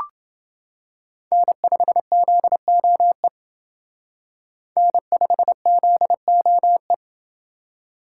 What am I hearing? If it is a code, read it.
Morse code, heard as a keyed tone, about 29 wpm, 700 Hz. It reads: N5ZOE N5ZOE